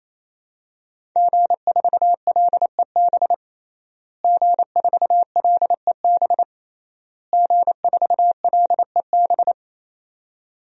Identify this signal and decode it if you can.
Morse code, heard as a keyed tone, about 28 wpm, 700 Hz. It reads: G4LEB G4LEB G4LEB
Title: DXCC-100 Call Signs - 5 Characters - Repeated 3x - 28wpm